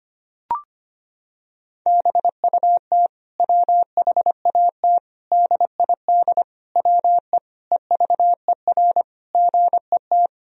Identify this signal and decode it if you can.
Morse code, heard as a keyed tone, about 25 wpm, 700 Hz. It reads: BUT WHAT DID WE EVER GET
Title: Sentences from Top 400 Words and Abbreviations 25wpm